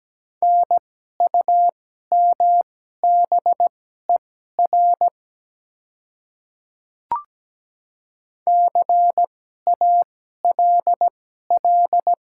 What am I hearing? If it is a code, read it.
Morse code, heard as a keyed tone, about 17 wpm, spaced as for 12 wpm, 700 Hz. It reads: NUMBER CALL
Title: 2x Character Spacing - Top 100 Words - 17wpm